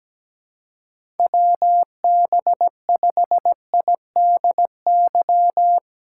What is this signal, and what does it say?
Morse code, heard as a keyed tone, about 17 wpm, 700 Hz. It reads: WB5IDY
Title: K1USN SST QSO Practice - 17wpm